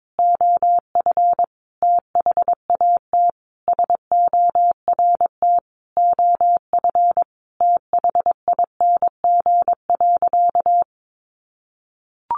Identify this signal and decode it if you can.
Morse code, heard as a keyed tone, about 22 wpm, 700 Hz. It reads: OF THAT SORT OF THING.
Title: Sentences from Top 1300 Words 22wpm